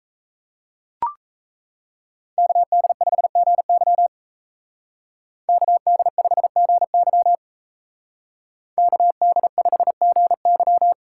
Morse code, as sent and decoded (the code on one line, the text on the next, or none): -.- -.. ..... --. -.-- / -.- -.. ..... --. -.-- / -.- -.. ..... --. -.--
KD5GY KD5GY KD5GY